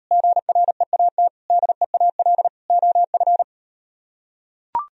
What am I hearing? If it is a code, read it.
Morse code, heard as a keyed tone, about 38 wpm, 700 Hz. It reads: GREAT DEAL OF